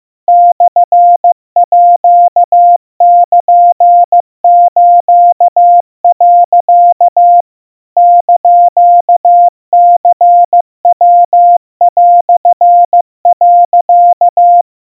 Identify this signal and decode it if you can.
Morse code (Japanese Wabun code), heard as a keyed tone, about 15 wpm, 700 Hz. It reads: モールス、 <KN>ニヤ)、
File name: iroha 50 cpm 700 Hz Crispy